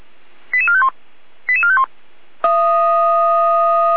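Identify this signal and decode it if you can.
Five-tone selective call (ZVEI), heard as quick runs of five tones, tones of about 70 ms. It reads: 88451 88451
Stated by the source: ZVEI five-tone selective call received on an SDR, logged by Feuerwehr Orlamünde (fire brigade).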